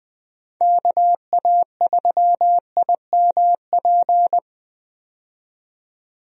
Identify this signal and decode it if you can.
Morse code, heard as a keyed tone, about 20 wpm, 700 Hz. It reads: KA3IMP
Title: USA Call Signs - 6 Characters - 20wpm